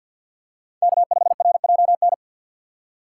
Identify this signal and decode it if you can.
Morse code, heard as a keyed tone, about 51 wpm, 700 Hz. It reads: K5RJN
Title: USA Call Signs - 5 Characters - 2x Character Spacing - 50wpm